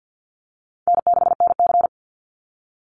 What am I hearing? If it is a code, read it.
Morse code, heard as a keyed tone, about 50 wpm, 700 Hz. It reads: N6NC
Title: Easy Call Signs Repeated 3x - 50wpm